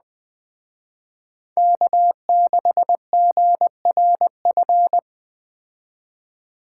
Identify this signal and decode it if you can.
Morse code, heard as a keyed tone, about 20 wpm, 700 Hz. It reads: K6GRF
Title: Speed Racing - USA Call Signs 17wpm